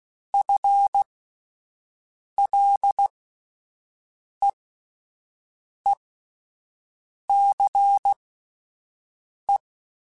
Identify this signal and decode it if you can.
Morse code, heard as a keyed tone, about 16 wpm, spaced as for 5 wpm, 800 Hz. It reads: FLEECE
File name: Spaced / FL Runs sp mp3s / FL Runs sp @10